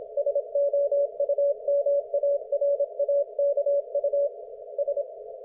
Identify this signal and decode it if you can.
Morse code, heard as a keyed tone, about 26 wpm, 570 Hz. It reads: SOUMARAKU S
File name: Archiv / CUC216 / ok1fhi cul216